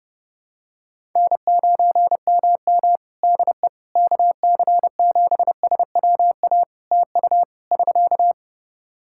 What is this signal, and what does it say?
Morse code, heard as a keyed tone, about 30 wpm, 700 Hz. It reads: N9MM DE KC7SWA TU <SK>
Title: Practice QSO - Exchange 4 - Ending - 30wpm